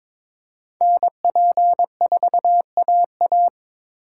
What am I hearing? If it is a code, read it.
Morse code, heard as a keyed tone, about 26 wpm, 700 Hz. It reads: NP4AA
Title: Speed Racing - USA 2x2 Call Signs 22wpm